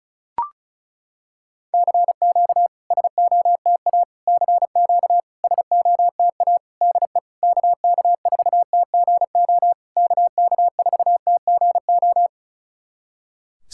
Morse code, heard as a keyed tone, about 36 wpm, 690 Hz. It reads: CQ SOTA CQ SOTA DE KK4TGO KK4TGO
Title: SOTA QSO Practice 35wpm